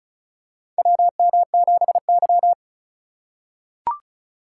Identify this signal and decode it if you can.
Morse code, heard as a keyed tone, about 36 wpm, 690 Hz. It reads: WM7Y
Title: USA Call Signs - 4 Characters - 2x Character Spacing - 35wpm